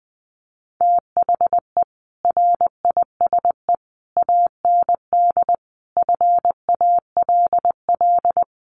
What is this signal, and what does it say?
Morse code, heard as a keyed tone, about 20 wpm, 700 Hz. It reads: THE RISE AND FALL
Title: Sets of 5 Words Encore2 20wpm